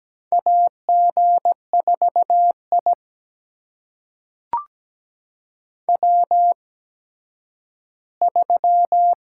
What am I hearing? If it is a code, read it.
Morse code, heard as a keyed tone, about 17 wpm, spaced as for 10 wpm, 700 Hz. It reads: AG4I W3
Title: USA Call Signs - 4 Characters - 8x Character Spacing - 17wpm